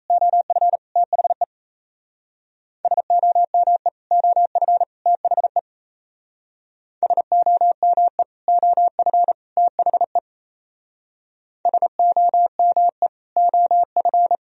Morse code, heard as a keyed tone, about 34 wpm, 700 Hz. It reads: OF THE SOME OF THE SOME OF THE SOME OF
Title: Speed-Racing - Sets of 3 Words 28wpm